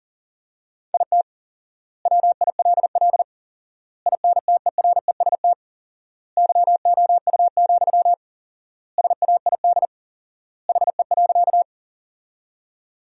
Morse code, heard as a keyed tone, about 40 wpm, 700 Hz. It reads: IT WILL INTEREST YOU, SAID HE.
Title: 4x Word-Spacing - Sentences from Top 400 Words - 40wpm